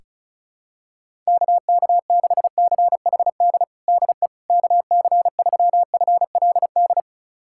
Morse code, heard as a keyed tone, about 34 wpm, 700 Hz. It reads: KK6CHD DE KC3FLD
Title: USA Callsign DE Callsign - 6 Characters - 35wpm